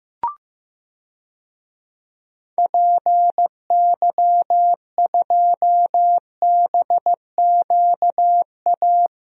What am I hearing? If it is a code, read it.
Morse code, heard as a keyed tone, about 15 wpm, 700 Hz. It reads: PY2BQA
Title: DXCC-100 Call Signs - 6 Characters - Repeated 3x - 15wpm